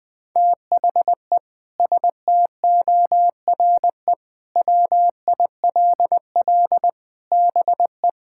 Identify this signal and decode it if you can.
Morse code, heard as a keyed tone, about 20 wpm, 700 Hz. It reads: THE STORE WILL BE
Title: Sentences with 6-Word Phrases - 20wpm